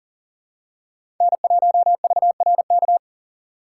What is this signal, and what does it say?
Morse code, heard as a keyed tone, about 40 wpm, 700 Hz. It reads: N1VRK
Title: USA Call Signs - 5 Characters - 6x Character Spacing - 40wpm